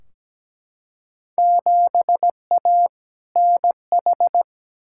Morse code, heard as a keyed tone, about 17 wpm, 700 Hz. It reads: 7A NH
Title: Field Day Exchanges - 17wpm